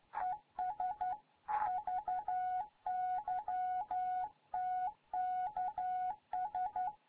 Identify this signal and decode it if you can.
Morse code, heard as a keyed tone, about 12 wpm, 740 Hz. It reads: ES VY TKS
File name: Morze-HA2MN